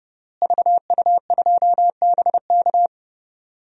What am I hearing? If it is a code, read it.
Morse code, heard as a keyed tone, about 32 wpm, 700 Hz. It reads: VU2BK
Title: DXCC-100 Call Signs - 5 Characters - Repeated 3x - 30wpm